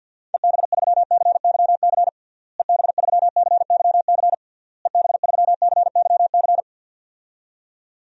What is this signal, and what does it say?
Morse code, heard as a keyed tone, about 51 wpm, 700 Hz. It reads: EB3CYC EB3CYC EB3CYC
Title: DXCC-100 Call Signs - 6 Characters - Repeated 3x - 50wpm